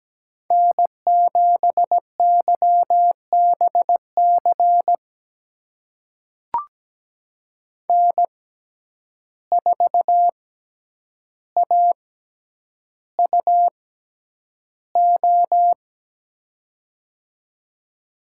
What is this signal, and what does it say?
Morse code, heard as a keyed tone, about 17 wpm, spaced as for 9 wpm, 700 Hz. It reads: N7YBC N4AUO
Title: USA Call Signs - 5 Characters - 6x Character Spacing - 17wpm